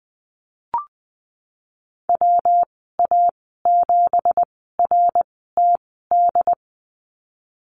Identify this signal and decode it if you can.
Morse code, heard as a keyed tone, about 19 wpm, spaced as for 14 wpm, 700 Hz. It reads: WA7RTD
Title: 167 Lesson 27g - Number 7 - Call Signs 20wpm